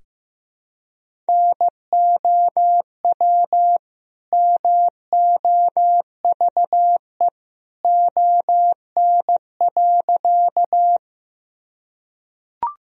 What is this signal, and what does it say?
Morse code, heard as a keyed tone, about 15 wpm, 700 Hz. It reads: NOW MOVE ON.
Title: Sentences from Top 1000 Words - Review Part 1 - 15wpm